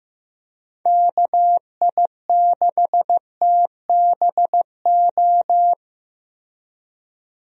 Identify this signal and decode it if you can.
Morse code, heard as a keyed tone, about 15 wpm, 700 Hz. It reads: KI6TBO
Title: USA Call Signs - 6 Characters - Repeated 3x - 15wpm